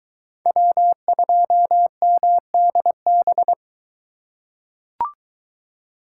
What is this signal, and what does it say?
Morse code, heard as a keyed tone, about 21 wpm, 700 Hz. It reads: W2MDB